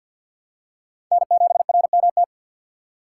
Morse code, heard as a keyed tone, about 50 wpm, 700 Hz. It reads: N7RMT
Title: USA Call Signs - 5 Characters - 4x Character Spacing - 50wpm